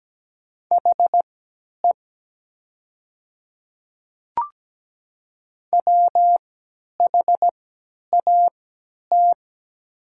Morse code, heard as a keyed tone, about 17 wpm, spaced as for 10 wpm, 700 Hz. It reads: HE WHAT
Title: 3x Character Spacing - Top 200 Words - 17wpm